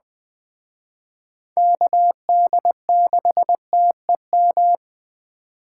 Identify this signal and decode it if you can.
Morse code, heard as a keyed tone, about 20 wpm, 700 Hz. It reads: KD6TEM
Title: Speed Racing - USA 2x3 Call Signs 17wpm